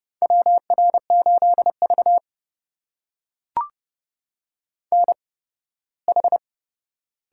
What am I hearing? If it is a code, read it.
Morse code, heard as a keyed tone, about 28 wpm, spaced as for 18 wpm, 700 Hz. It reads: WR8V NH
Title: USA Call Signs - 4 Characters - 8x Character Spacing - 30wpm